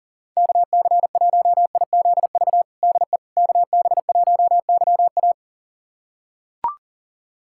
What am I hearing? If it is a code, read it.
Morse code, heard as a keyed tone, about 40 wpm, 700 Hz. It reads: KC1IZV DE KB1YA
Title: USA Callsign DE Callsign 40wpm